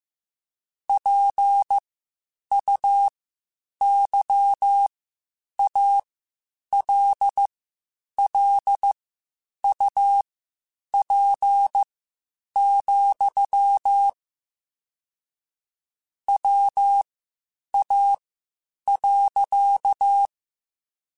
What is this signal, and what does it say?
Morse code, heard as a keyed tone, about 15 wpm, spaced as for 8 wpm, 800 Hz. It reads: PUYALLUP, WA.